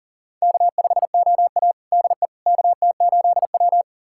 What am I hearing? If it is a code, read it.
Morse code, heard as a keyed tone, about 40 wpm, 700 Hz. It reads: K5OA DE KT8W
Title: USA Callsign DE Callsign - 4 Characters - 40wpm